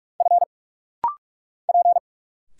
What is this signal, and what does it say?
Morse code, heard as a keyed tone, about 46 wpm, 700 Hz. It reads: F P
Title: Single Letters - Rapid-Fire 45wpm